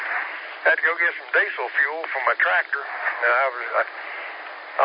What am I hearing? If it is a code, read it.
Emergency Alert System (SAME) tones, heard as harsh buzzing bursts, with speech. It is heard from 1.0 to 4.4 s.